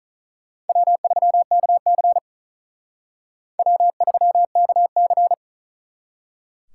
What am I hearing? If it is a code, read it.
Morse code, heard as a keyed tone, about 45 wpm, 700 Hz. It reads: W3KC W3KC